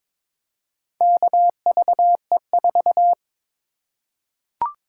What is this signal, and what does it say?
Morse code, heard as a keyed tone, about 22 wpm, 700 Hz. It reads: KVE4
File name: practice / 22wpm / Four Letter-Number Combinations - 22wpm